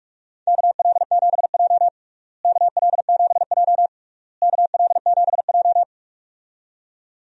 Morse code, heard as a keyed tone, about 48 wpm, 700 Hz. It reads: KL7J KL7J KL7J
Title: DXCC-100 Call Signs - 4 Characters - Repeated 3x - 45wpm